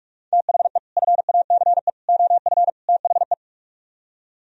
Morse code, heard as a keyed tone, about 47 wpm, 700 Hz. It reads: THE FACE OF THE